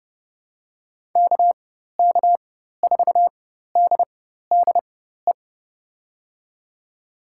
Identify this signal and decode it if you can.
Morse code, heard as a keyed tone, about 30 wpm, spaced as for 14 wpm, 700 Hz. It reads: KK4DDE
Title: USA Call Signs - 6 Characters - 4x Character Spacing - 30wpm